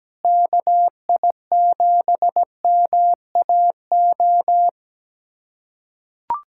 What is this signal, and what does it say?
Morse code, heard as a keyed tone, about 17 wpm, 700 Hz. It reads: KI7MAO
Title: USA Callsign DE Callsign 17wpm